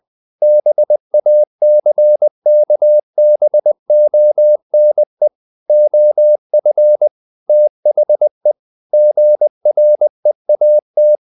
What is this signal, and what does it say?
Morse code, heard as a keyed tone, about 20 wpm, 600 Hz. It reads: BACKBONE OF THE GREAT
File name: the old man and the sea part 2 20wpm